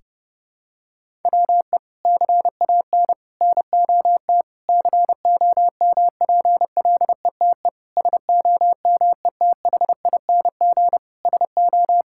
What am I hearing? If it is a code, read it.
Morse code, heard as a keyed tone, about 30 wpm, 700 Hz. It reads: WE CAN NOT COMPLETE SOMETHING SO
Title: Sentences from Top 1000 Words - Review Part 6 - 30wpm